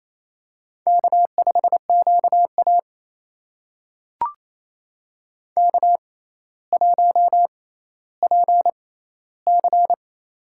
Morse code, heard as a keyed tone, about 28 wpm, spaced as for 14 wpm, 700 Hz. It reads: K5QA K1PC
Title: USA Call Signs - 4 Characters - 6x Character Spacing - 28wpm